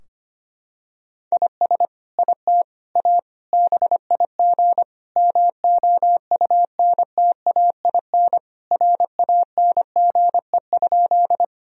Morse code, heard as a keyed tone, about 25 wpm, 700 Hz. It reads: IS IT A BIG MOUNTAIN RANGE?
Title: Sentences from Top 1000 Words - Review Part 4 - 25wpm